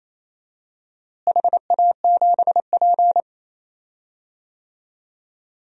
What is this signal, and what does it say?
Morse code, heard as a keyed tone, about 28 wpm, 700 Hz. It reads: HA7P
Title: DXCC-100 Call Signs - 4 Characters - Repeated 3x - 28wpm